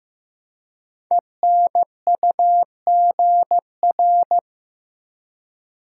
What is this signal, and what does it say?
Morse code, heard as a keyed tone, about 15 wpm, 700 Hz. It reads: ENUGR
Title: Code Groups - 5 Letters - 15wpm